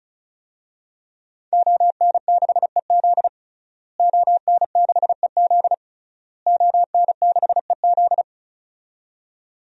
Morse code, heard as a keyed tone, about 36 wpm, 700 Hz. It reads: ON6EZ ON6EZ ON6EZ
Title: DXCC-100 Call Signs - 5 Characters - Repeated 3x - 35wpm